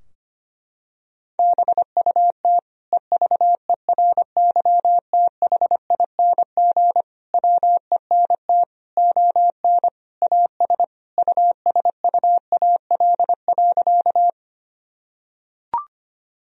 Morse code, heard as a keyed tone, about 25 wpm, 700 Hz. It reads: BUT EVERYTHING WENT ON AS USUAL.
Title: Sentences from Top 1000 Words - Review Part 2 - 25wpm